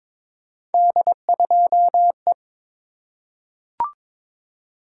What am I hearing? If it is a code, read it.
Morse code, heard as a keyed tone, about 23 wpm, 700 Hz. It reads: D2E
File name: practice / 22wpm / Three Letter-Number Combinations - 22wpm